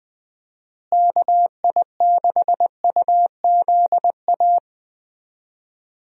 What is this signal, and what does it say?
Morse code, heard as a keyed tone, about 20 wpm, 700 Hz. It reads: KI6UZA
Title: USA Call Signs - 6 Characters - Repeated 2x - 20wpm